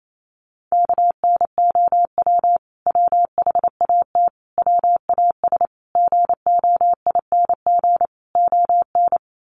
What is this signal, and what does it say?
Morse code, heard as a keyed tone, about 28 wpm, 700 Hz. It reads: KNOW WHAT WAS GOING ON
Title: Sets of 5 Words Encore2 28wpm